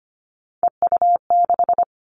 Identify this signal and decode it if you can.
Morse code, heard as a keyed tone, about 25 wpm, 700 Hz. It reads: EU6
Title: Callsign Prefixes for Top 100 DXCC - 25wpm